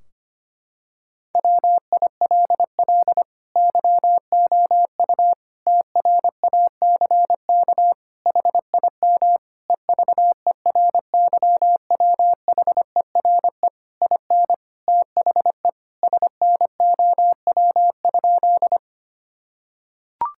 Morse code, heard as a keyed tone, about 25 wpm, 700 Hz. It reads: WILL YOU TRACK HIM EVERYWHERE IN THE SNOW?
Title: Sentences from Top 1600 Words 25wpm